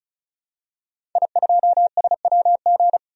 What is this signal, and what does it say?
Morse code, heard as a keyed tone, about 34 wpm, 700 Hz. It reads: I2SWG
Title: DXCC-100 Call Signs - 5 Characters - 35wpm